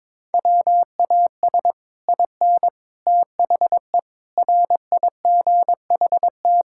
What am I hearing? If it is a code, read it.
Morse code, heard as a keyed tone, about 22 wpm, 700 Hz. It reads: WAS IN THE RIGHT
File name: Sets of 9 Words - 22wpm